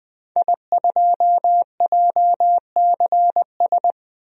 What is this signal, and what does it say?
Morse code, heard as a keyed tone, about 21 wpm, 700 Hz. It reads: I2JCS
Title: DXCC-100 Call Signs - 5 Characters - Repeated 2x - 20wpm